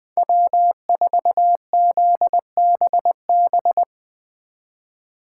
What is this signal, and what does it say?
Morse code, heard as a keyed tone, about 22 wpm, 700 Hz. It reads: W4ZBB
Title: Speed Racing - USA 1x3 Call Signs 20wpm